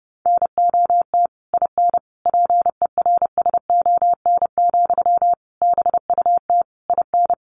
Morse code, heard as a keyed tone, about 30 wpm, 700 Hz. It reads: NOT IN PERSON, BUT IN
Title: Sentences from Top 1400 Words 30wpm